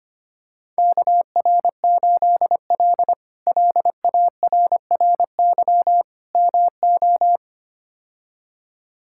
Morse code, heard as a keyed tone, about 25 wpm, 700 Hz. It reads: KR8L LARRY MO
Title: K1USN SST QSO Practice - 25wpm